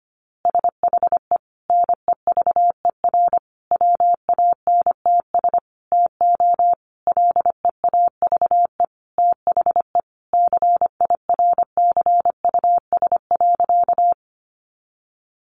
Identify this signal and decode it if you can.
Morse code, heard as a keyed tone, about 24 wpm, 700 Hz. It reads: SHE NEVER WANTS TO LEAVE THE CIRCUS.
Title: Easy Phrases 25wpm